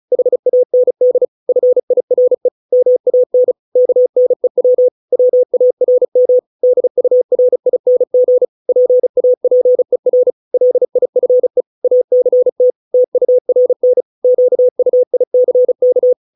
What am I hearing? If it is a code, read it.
Morse code, heard as a keyed tone, about 34 wpm, 500 Hz. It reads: HAND FIRE MAN KNEW WARM DURING PAPER LIFE ACT TURN QUICK